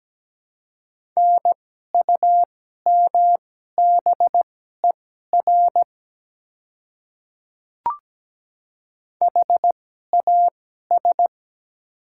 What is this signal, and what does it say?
Morse code, heard as a keyed tone, about 17 wpm, spaced as for 12 wpm, 700 Hz. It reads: NUMBER HAS
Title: 2x Character Spacing - Top 100 Words - 17wpm